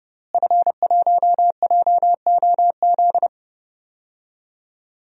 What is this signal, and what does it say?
Morse code, heard as a keyed tone, about 32 wpm, 700 Hz. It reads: F1JOZ
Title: DXCC-100 Call Signs - 5 Characters - Repeated 3x - 30wpm